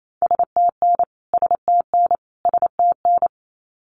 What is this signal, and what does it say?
Morse code, heard as a keyed tone, about 28 wpm, 700 Hz. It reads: STN STN STN
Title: Abbreviations Repeated 3x - 28wpm